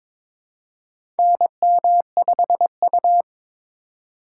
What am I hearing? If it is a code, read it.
Morse code, heard as a keyed tone, about 22 wpm, 700 Hz. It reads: NM5U
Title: USA Call Signs - 4 Characters - 2x Character Spacing - 22wpm